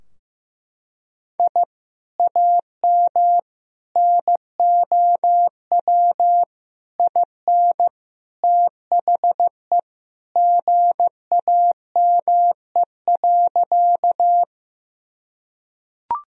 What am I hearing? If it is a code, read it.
Morse code, heard as a keyed tone, about 15 wpm, 700 Hz. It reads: I AM NOW IN THE GAME.
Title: Sentences from Top 1000 Words - Review Part 3 - 15wpm